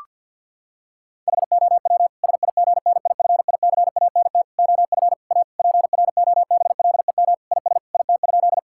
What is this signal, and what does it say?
Morse code, heard as a keyed tone, about 50 wpm, 700 Hz. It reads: HOW SIGNIFICANT OF A PROBLEM IS IT?
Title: Sentences from Top 1000 Words - Review Part 6 - 50wpm